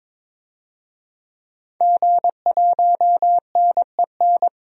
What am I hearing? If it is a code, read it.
Morse code, heard as a keyed tone, about 22 wpm, 700 Hz. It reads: G1NEN
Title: DXCC-100 Call Signs - 5 Characters - 22wpm